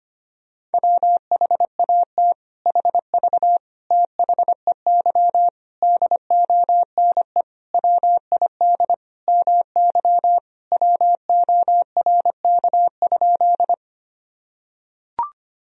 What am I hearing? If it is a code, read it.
Morse code, heard as a keyed tone, about 25 wpm, 700 Hz. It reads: WHAT HV THEY DONE WID MY WORK?
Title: Sentences from Top 500 Words and Abbreviations 25wpm